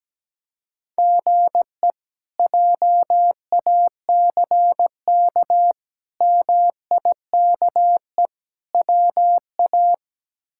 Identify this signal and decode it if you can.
Morse code, heard as a keyed tone, about 17 wpm, 700 Hz. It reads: GE JACK MIKE WA
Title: K1USN SST QSO Practice - 17wpm